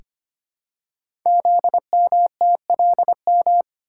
Morse code, heard as a keyed tone, about 25 wpm, 700 Hz. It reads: ZMTLM
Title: Code Groups - 5 Letters - 25wpm